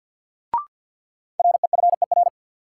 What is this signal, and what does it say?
Morse code, heard as a keyed tone, about 50 wpm, 700 Hz. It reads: REFER
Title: Sentences from Top 1000 Words - New Word Drill - 50wpm